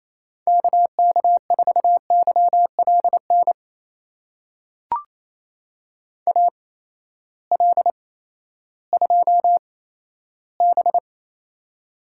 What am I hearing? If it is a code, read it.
Morse code, heard as a keyed tone, about 28 wpm, spaced as for 14 wpm, 700 Hz. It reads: KK4YLN AL2B
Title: USA Call Signs - 8x Character Spacing - 28wpm